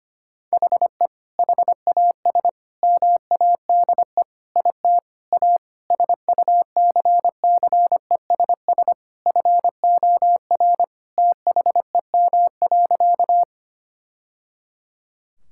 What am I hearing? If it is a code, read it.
Morse code, heard as a keyed tone, about 25 wpm, 700 Hz. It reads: HE HAS MADE IT A SUCCESS FOR THEM.
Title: Sentences from Top 1000 Words - Review Part 6 - 25wpm